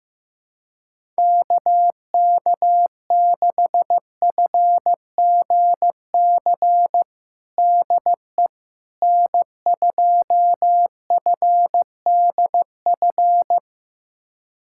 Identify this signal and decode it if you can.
Morse code, heard as a keyed tone, about 15 wpm, 700 Hz. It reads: KK6FGC DE N2FDF
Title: USA Callsign DE Callsign 15wpm